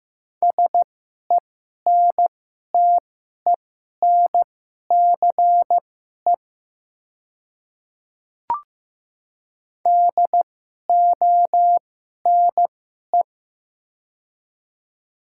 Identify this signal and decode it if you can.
Morse code, heard as a keyed tone, about 15 wpm, spaced as for 11 wpm, 690 Hz. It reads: SENTENCE DONE
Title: 2x Character Spacing - Top 500 Words - 15wpm